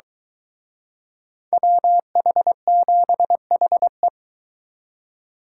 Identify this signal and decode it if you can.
Morse code, heard as a keyed tone, about 23 wpm, 700 Hz. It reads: WH7HE